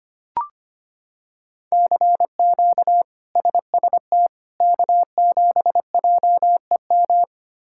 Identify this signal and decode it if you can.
Morse code, heard as a keyed tone, about 25 wpm, 700 Hz. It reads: CQ SST K7JEM
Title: K1USN SST QSO Practice - 25wpm